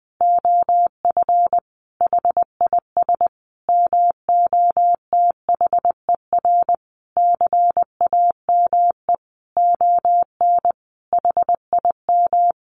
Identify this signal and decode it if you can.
Morse code, heard as a keyed tone, about 19 wpm, 700 Hz. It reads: OF HIS MOTHER CAME ON HIM
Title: Sentences from Top 1100 Words 20wpm